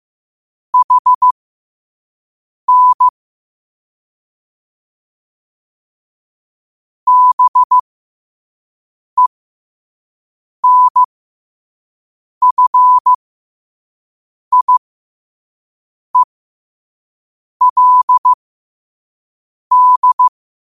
Morse code, heard as a keyed tone, about 15 wpm, spaced as for 5 wpm, 1000 Hz. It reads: HN BENFIELD
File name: qotdfiles / 5wpm